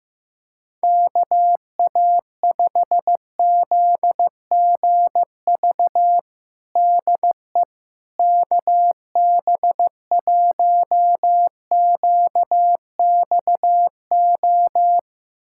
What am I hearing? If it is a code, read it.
Morse code, heard as a keyed tone, about 15 wpm, 700 Hz. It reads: KA5ZGV DE KB1QXO
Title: USA Callsign DE Callsign - 6 Characters - 15wpm